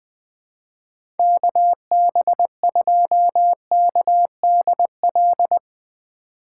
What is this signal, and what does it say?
Morse code, heard as a keyed tone, about 20 wpm, 700 Hz. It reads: KB2KDL
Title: USA Call Signs - 6 Characters - 3x Character Spacing - 20wpm